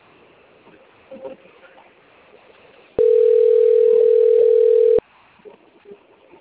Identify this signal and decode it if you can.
Ringback tone (North America), 440 + 480 Hz.
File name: out- - - - - .7621